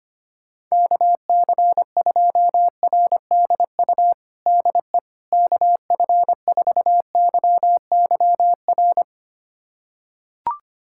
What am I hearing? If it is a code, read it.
Morse code, heard as a keyed tone, about 25 wpm, 700 Hz. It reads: KC2RDU DE KF4YYR
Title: USA Callsign DE Callsign 25wpm